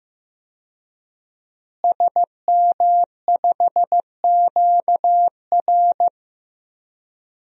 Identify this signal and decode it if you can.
Morse code, heard as a keyed tone, about 15 wpm, 700 Hz. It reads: SM5QR
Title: DXCC-100 Call Signs - 5 Characters - Repeated 3x - 15wpm